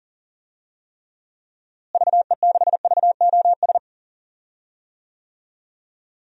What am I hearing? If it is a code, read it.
Morse code, heard as a keyed tone, about 40 wpm, 700 Hz. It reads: VE6VOS
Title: DXCC-100 Call Signs - 6 Characters - Repeated 2x - 40wpm